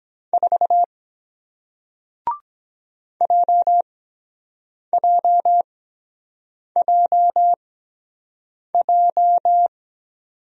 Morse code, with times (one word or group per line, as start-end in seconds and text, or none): 0.3-0.8 4
3.2-9.7 JJJJ